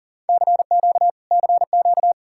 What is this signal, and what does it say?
Morse code, heard as a keyed tone, about 53 wpm, 700 Hz. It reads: CQ CQ